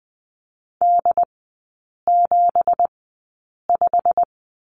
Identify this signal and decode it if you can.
Morse code, heard as a keyed tone, about 20 wpm, spaced as for 8 wpm, 700 Hz. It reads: D75
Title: 166 Lesson 27e - Number 7 - Three Characters 20wpm